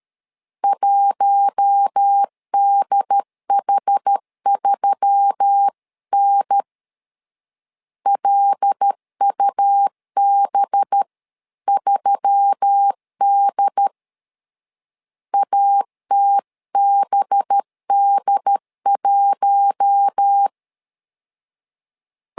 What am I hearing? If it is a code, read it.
Morse code, heard as a keyed tone, about 13 wpm, 800 Hz. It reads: 1DH3N LUB3D ATBD1
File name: LEZIONI / 5 LBVC / 5 LB 12wpm 1906